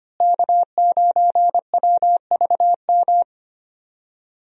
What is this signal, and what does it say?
Morse code, heard as a keyed tone, about 25 wpm, 700 Hz. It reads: K9WVM
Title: USA Call Signs - 5 Characters - Repeated 3x - 25wpm